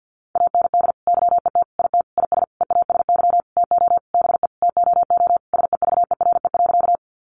Morse code, heard as a keyed tone, about 50 wpm, 700 Hz. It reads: AND YET IT IS EASY TO BE TOO SEVERE.
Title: Sentences from Top 1400 Words 50wpm